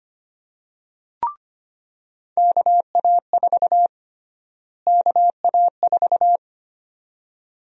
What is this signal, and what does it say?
Morse code, heard as a keyed tone, about 25 wpm, 700 Hz. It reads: KA4 KA4
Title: USA Call Sign Prefixes - Repeated 2x - 25wpm